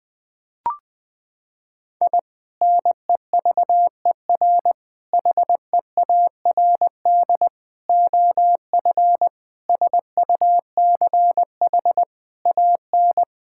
Morse code, heard as a keyed tone, about 20 wpm, 700 Hz. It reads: I NEVER HEARD OF SUCH AN
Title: Sentences from Top 1000 Words - Review Part 5 - 20wpm